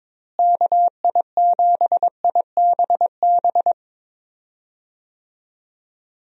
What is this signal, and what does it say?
Morse code, heard as a keyed tone, about 22 wpm, 700 Hz. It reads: KI7IBB
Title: USA Call Signs - 6 Characters - Repeated 2x - 22wpm